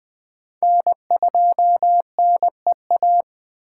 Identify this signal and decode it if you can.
Morse code, heard as a keyed tone, about 20 wpm, 700 Hz. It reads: N2NEA